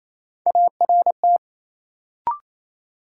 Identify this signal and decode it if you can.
Morse code, heard as a keyed tone, about 28 wpm, 700 Hz. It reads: ART
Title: Sentences from Top 600 Words - New Word Drill - 28wpm